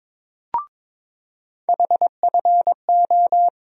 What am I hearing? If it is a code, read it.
Morse code, heard as a keyed tone, about 22 wpm, 700 Hz. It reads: HFO
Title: USA Call Sign Postfixes - 3 Letters 22wpm